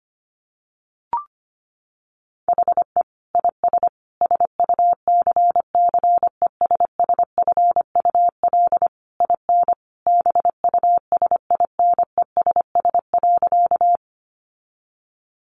Morse code, heard as a keyed tone, about 25 wpm, 700 Hz. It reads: HE IS SUCCESSFUL IN BUSINESS.
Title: Sentences from Top 900 Words 25wpm